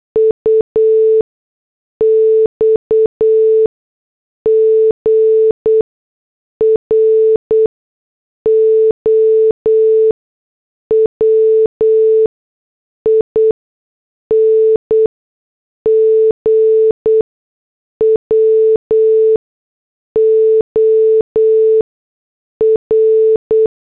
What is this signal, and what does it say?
Morse code, heard as a keyed tone, about 8 wpm, spaced as for 6 wpm, 440 Hz. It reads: UX GROWING WOR